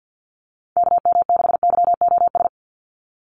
Morse code, heard as a keyed tone, about 50 wpm, 700 Hz. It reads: KM6YOS
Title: USA Call Signs 50wpm